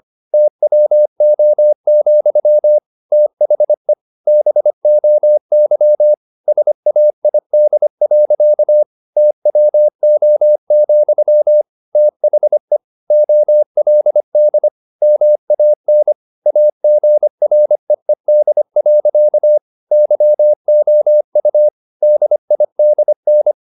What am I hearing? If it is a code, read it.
Morse code, heard as a keyed tone, about 25 wpm, 600 Hz. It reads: TWO, THE BOY SAID. TWO, THE OLD MAN AGREED. YOU DIDN
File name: the old man and the sea part 1 25wpm